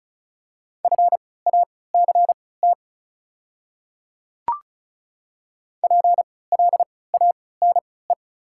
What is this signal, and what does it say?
Morse code, heard as a keyed tone, about 34 wpm, spaced as for 20 wpm, 690 Hz. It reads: FACT PLANE
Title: 3x Character Spacing - Top 500 Words - 35wpm